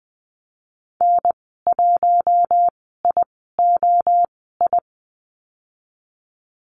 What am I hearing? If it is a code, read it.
Morse code, heard as a keyed tone, about 20 wpm, spaced as for 15 wpm, 700 Hz. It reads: N1IOI
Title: 076 Lesson 13g - Letter L - Call Signs 20wpm